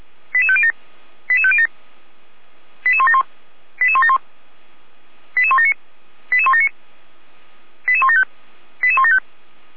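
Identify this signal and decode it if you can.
Five-tone selective call (ZVEI), heard as quick runs of five tones, tones of about 70 ms. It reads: 88507 88507 88171 88171 88179 88179 88175 88175
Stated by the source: ZVEI five-tone call 88191 received by SDR.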